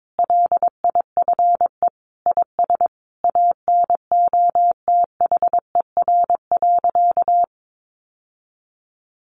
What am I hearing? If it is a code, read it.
Morse code, heard as a keyed tone, about 22 wpm, 700 Hz. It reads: LIFE IS ANOTHER.
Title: Sentences from Top 1000 Words 22wpm